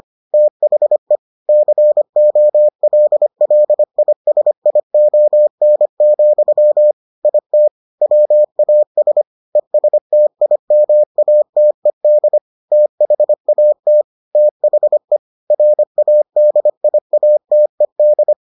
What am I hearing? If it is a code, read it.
Morse code, heard as a keyed tone, about 25 wpm, 600 Hz. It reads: THE COLLISION, IT WAS ESTIMATED THAT THE RADIATED